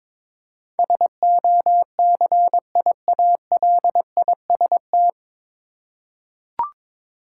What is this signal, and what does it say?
Morse code, heard as a keyed tone, about 22 wpm, 700 Hz. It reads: SOCIALIST
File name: 9 Letter Words - First 3 Letters Given - 22wpm